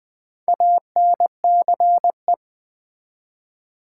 Morse code, heard as a keyed tone, about 20 wpm, 700 Hz. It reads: ANCE
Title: Build A Word - 2 Letters At A Time - 8-Letter Words - 20wpm